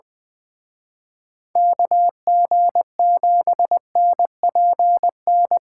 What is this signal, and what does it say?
Morse code, heard as a keyed tone, about 24 wpm, 700 Hz. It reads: KG7NPN
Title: Speed Racing - USA 2x3 Call Signs 20wpm